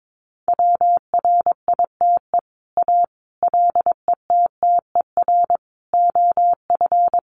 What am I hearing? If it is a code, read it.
Morse code, heard as a keyed tone, about 22 wpm, 700 Hz. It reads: WRITE A LETTER OF
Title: Sentences from Top 1400 Words 22wpm